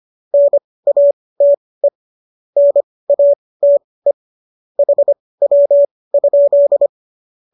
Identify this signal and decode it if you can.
Morse code, heard as a keyed tone, about 25 wpm, spaced as for 18 wpm, 570 Hz. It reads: NATE NATE HW?
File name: qso211 18